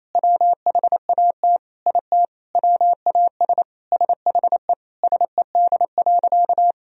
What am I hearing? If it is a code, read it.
Morse code, heard as a keyed tone, about 28 wpm, 700 Hz. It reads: WHAT IT WAS SHE SED.
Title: Sentences from Top 400 Words and Abbreviations 28wpm